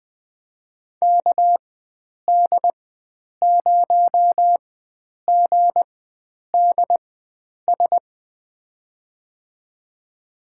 Morse code, heard as a keyed tone, about 20 wpm, spaced as for 9 wpm, 700 Hz. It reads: KD0GDS
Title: USA Call Signs - 6 Characters - 4x Character Spacing - 20wpm